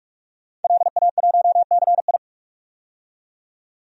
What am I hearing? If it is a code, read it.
Morse code, heard as a keyed tone, about 44 wpm, 700 Hz. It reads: LA1CI
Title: DXCC-100 Call Signs - 5 Characters - 45wpm